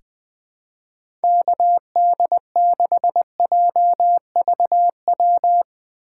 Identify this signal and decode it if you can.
Morse code, heard as a keyed tone, about 20 wpm, 700 Hz. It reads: KD6JVW
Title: USA Call Signs - 6 Characters - 4x Character Spacing - 20wpm